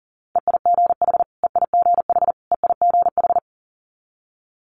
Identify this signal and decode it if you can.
Morse code, heard as a keyed tone, about 38 wpm, 700 Hz. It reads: EIGH EIGH EIGH
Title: Multi-letter Phonemes Repeated 3x 40wpm